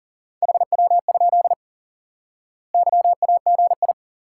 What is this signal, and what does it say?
Morse code, heard as a keyed tone, about 37 wpm, 700 Hz. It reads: HW? YAGI
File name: IQE Course 3 - Lesson 89 - YAGI - Two Elements - 40wpm